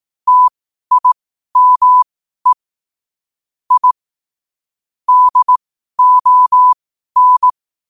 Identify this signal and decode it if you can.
Morse code, heard as a keyed tone, about 18 wpm, spaced as for 12 wpm, 1000 Hz. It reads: TIME I DON
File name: qotdfiles / 12wpm